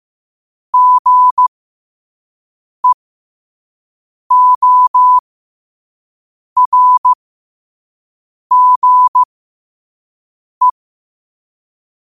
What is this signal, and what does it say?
Morse code, heard as a keyed tone, about 15 wpm, spaced as for 5 wpm, 1000 Hz. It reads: GEORGE
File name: qotdfiles / 5wpm